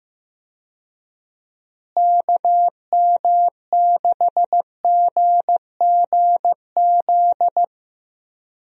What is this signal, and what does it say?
Morse code, heard as a keyed tone, about 15 wpm, 700 Hz. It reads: KM6GGZ
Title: USA Call Signs - 6 Characters - Repeated 2x - 15wpm